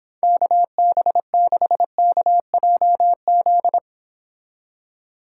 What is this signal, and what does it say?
Morse code, heard as a keyed tone, about 26 wpm, 700 Hz. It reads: KB6KJZ